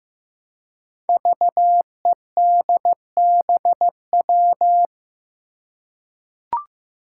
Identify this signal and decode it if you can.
Morse code, heard as a keyed tone, about 15 wpm, 700 Hz. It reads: VEDBW
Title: Five Challenging Letter-Number Combinations 15wpm